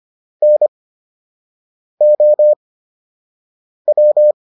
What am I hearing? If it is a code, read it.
Morse code, heard as a keyed tone, about 25 wpm, spaced as for 6 wpm, 600 Hz. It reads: NOW